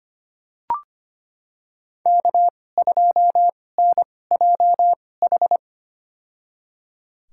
Morse code, heard as a keyed tone, about 25 wpm, spaced as for 18 wpm, 700 Hz. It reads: K2NJH
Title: USA Call Signs - 2x Character Spacing - 25wpm